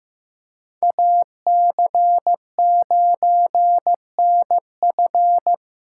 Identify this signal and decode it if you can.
Morse code, heard as a keyed tone, about 18 wpm, 700 Hz. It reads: AC9NF
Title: Speed Racing - USA Call Signs 15wpm